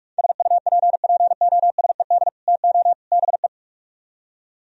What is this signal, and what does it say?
Morse code, heard as a keyed tone, about 64 wpm, spaced as for 54 wpm, 700 Hz. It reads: SUPPOSED TO BE
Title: Speed-Racing - Sets of 3 Words 45wpm